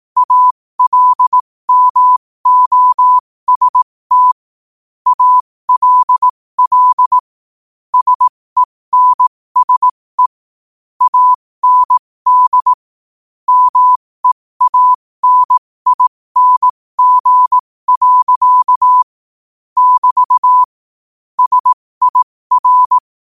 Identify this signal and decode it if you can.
Morse code, heard as a keyed tone, about 18 wpm, spaced as for 15 wpm, 1000 Hz. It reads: ALMOST ALL SENSE AND MEANING. <BT> SIR
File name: qotdfiles / 15wpm